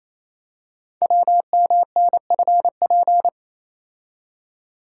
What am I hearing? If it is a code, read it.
Morse code, heard as a keyed tone, about 29 wpm, 700 Hz. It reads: WMNFP
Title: Code Groups - 5 Letters - 28wpm